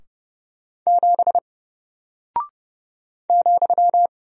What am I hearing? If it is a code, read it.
Morse code, heard as a keyed tone, about 30 wpm, 700 Hz. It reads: 7 ,
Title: Single Character (Letter:Number:Punctuation:Prosign) Alt 30wpm